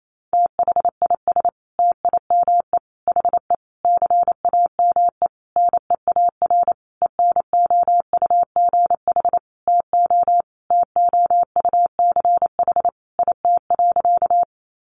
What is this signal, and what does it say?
Morse code, heard as a keyed tone, about 28 wpm, 700 Hz. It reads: THIS TIME HE CAME NEAR ENOUGH TO TOUCH IT.
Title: Sentences from Top 1200 Words 28wpm